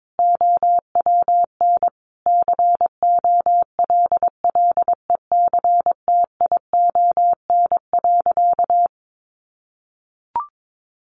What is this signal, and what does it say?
Morse code, heard as a keyed tone, about 22 wpm, 700 Hz. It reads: OWN COLLECTION.
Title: Sentences from Top 1200 Words 22wpm